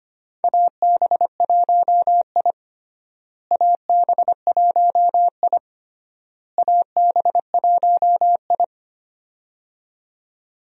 Morse code, heard as a keyed tone, about 25 wpm, 700 Hz. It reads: AB1I AB1I AB1I
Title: USA Call Signs - 4 Characters - Repeated 3x - 25wpm